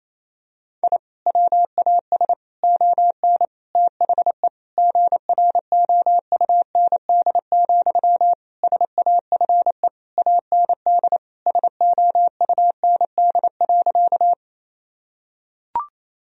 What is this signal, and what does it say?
Morse code, heard as a keyed tone, about 28 wpm, 700 Hz. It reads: I WAS ON THE GROUND, SAFE AND SOUND.
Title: Sentences from Top 1000 Words - Review Part 2 - 28wpm